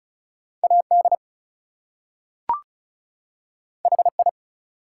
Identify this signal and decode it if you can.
Morse code, heard as a keyed tone, about 36 wpm, 700 Hz. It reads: AD HI
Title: Two Letter Words - 35wpm